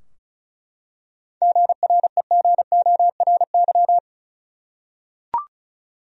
Morse code, heard as a keyed tone, about 34 wpm, 700 Hz. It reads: GREGORY
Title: USA Common Names - 35wpm